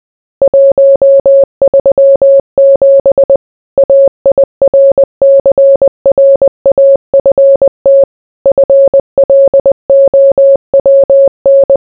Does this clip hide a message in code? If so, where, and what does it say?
Yes: Morse code from 0.4 to 11.8 s, heard as a keyed tone, about 20 wpm, 550 Hz. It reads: 137 AIRCRAFT FLOWN